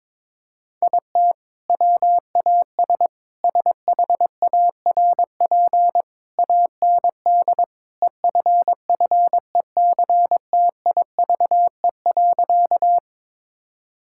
Morse code, heard as a keyed tone, about 22 wpm, 700 Hz. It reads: IT WAS SHARP AND EFFECTIVE.
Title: Sentences from Top 1600 Words 22wpm